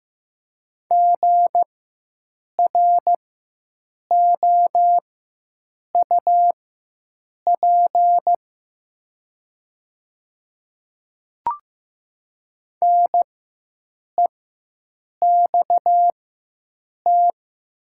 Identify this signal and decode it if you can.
Morse code, heard as a keyed tone, about 15 wpm, spaced as for 7 wpm, 700 Hz. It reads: GROUP NEXT
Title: 4x Character Spacing - Top 500 Words - 15wpm